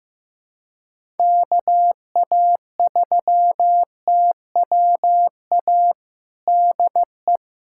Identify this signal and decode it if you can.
Morse code, heard as a keyed tone, about 15 wpm, 700 Hz. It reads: KA3TWA DE
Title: USA Callsign DE Callsign - 6 Characters - 15wpm